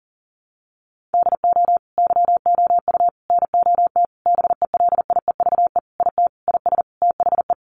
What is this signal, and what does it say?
Morse code, heard as a keyed tone, about 40 wpm, 700 Hz. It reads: DO YOU NOT BELIEVE IT IS THE
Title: Sentences from Top 1000 Words 40wpm